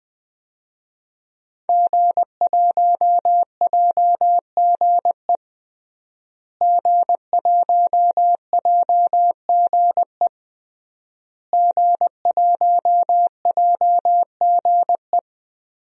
Morse code, heard as a keyed tone, about 21 wpm, 700 Hz. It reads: G1JGE G1JGE G1JGE
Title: DXCC-100 Call Signs - 5 Characters - Repeated 3x - 20wpm